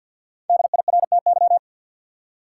Morse code, heard as a keyed tone, about 53 wpm, 700 Hz. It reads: DIRTY
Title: Five Letter Words - 50wpm